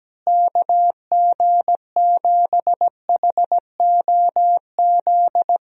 Morse code, heard as a keyed tone, about 17 wpm, 700 Hz. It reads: KG7HOZ
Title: USA Call Signs - 6 Characters - Repeated 3x - 17wpm